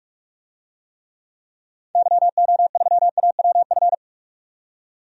Morse code, heard as a keyed tone, about 46 wpm, 700 Hz. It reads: YO3AWF
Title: DXCC-100 Call Signs - 6 Characters - 45wpm